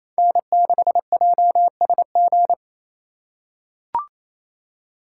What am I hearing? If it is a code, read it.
Morse code, heard as a keyed tone, about 28 wpm, 700 Hz. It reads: N6JSG
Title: USA Call Signs - 5 Characters - 2x Character Spacing - 28wpm